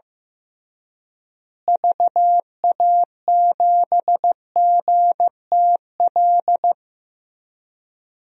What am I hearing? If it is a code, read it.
Morse code, heard as a keyed tone, about 15 wpm, 700 Hz. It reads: VA7GTL
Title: DXCC-100 Call Signs - 6 Characters - Repeated 3x - 15wpm